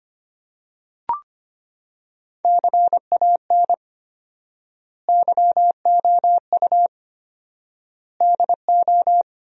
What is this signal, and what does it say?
Morse code, heard as a keyed tone, about 25 wpm, 700 Hz. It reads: CAN YOU DO
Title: 4x Word-Spacing - Sentences from Top 100 Words - 25wpm